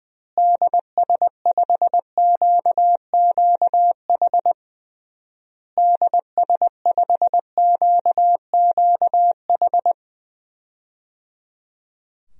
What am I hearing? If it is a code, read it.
Morse code, heard as a keyed tone, about 20 wpm, 700 Hz. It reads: DS5QQH DS5QQH
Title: DXCC-100 Call Signs - 6 Characters - Repeated 3x - 20wpm